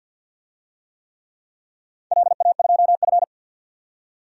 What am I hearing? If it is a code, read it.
Morse code, heard as a keyed tone, about 46 wpm, 700 Hz. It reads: LA2F
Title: DXCC-100 Call Signs - 4 Characters - 50wpm